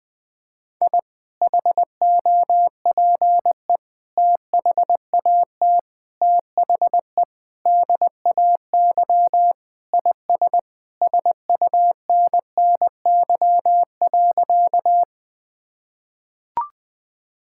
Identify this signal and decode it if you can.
Morse code, heard as a keyed tone, about 21 wpm, 700 Hz. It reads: I HOPE THAT THE DAY IS SUNNY.
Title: Sentences with 2-Word Phrases - Spoken Beginning - 20wpm